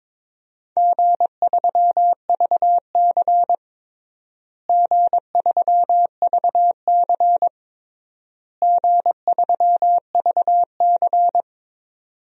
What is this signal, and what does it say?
Morse code, heard as a keyed tone, about 22 wpm, 700 Hz. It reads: G3VC G3VC G3VC
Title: DXCC-100 Call Signs - 4 Characters - Repeated 3x - 22wpm